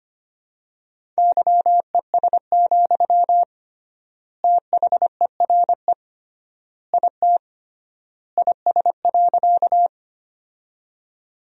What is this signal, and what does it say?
Morse code, heard as a keyed tone, about 25 wpm, 700 Hz. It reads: YES, THERE IT IS.
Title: 3x Word-Spacing - Sentences from Top 500 Words - 25wpm